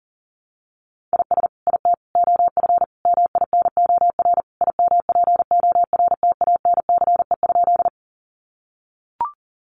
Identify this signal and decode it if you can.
Morse code, heard as a keyed tone, about 40 wpm, 700 Hz. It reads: IS IT OF MINOR IMPORTANCE?
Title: Sentences from Top 1500 Words 40wpm